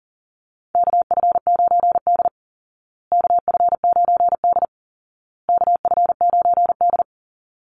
Morse code, heard as a keyed tone, about 40 wpm, 700 Hz. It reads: KF9D KF9D KF9D
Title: Call Signs Repeated 2x - 40wpm